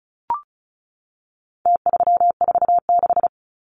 Morse code, heard as a keyed tone, about 34 wpm, 700 Hz. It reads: T346
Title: Numbers - Contest Style - 4-Digit 35wpm